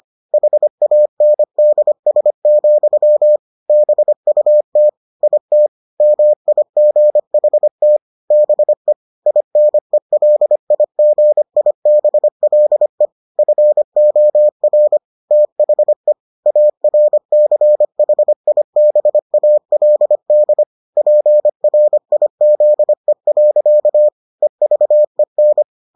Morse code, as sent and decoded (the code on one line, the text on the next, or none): .... .- -. -.. ... --..-- / -... ..- - / .. - / -- .. --. .... - / -... / .. -. .-.. .. --. .. -... .-.. / ..-. --- .-. / - .... / .- .-. -.-. .... .. -... .- .-.. -.. / .--. .-. .. --.. .-.-.- / . ...- . -.
HANDS, BUT IT MIGHT BE INELIGIBLE FOR THE ARCHIBALD PRIZE. EVEN